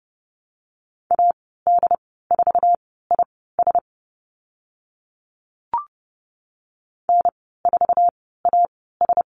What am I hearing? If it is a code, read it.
Morse code, heard as a keyed tone, about 30 wpm, spaced as for 17 wpm, 700 Hz. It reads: AD4IS N4AS
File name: Lesson 14g - Number 2 - Call Signs 30wpm